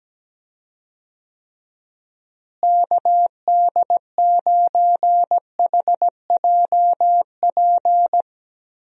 Morse code, heard as a keyed tone, about 17 wpm, 700 Hz. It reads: KD9HJP
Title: USA Call Signs - 6 Characters - Repeated 2x - 17wpm